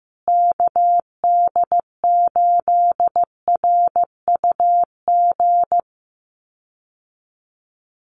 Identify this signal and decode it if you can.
Morse code, heard as a keyed tone, about 15 wpm, 700 Hz. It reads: KD8RUG
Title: USA Call Signs Repeated 2x 15wpm